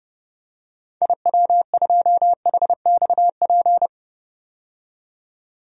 Morse code, heard as a keyed tone, about 32 wpm, 700 Hz. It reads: IW2HXP
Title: DXCC-100 Call Signs - 6 Characters - 30wpm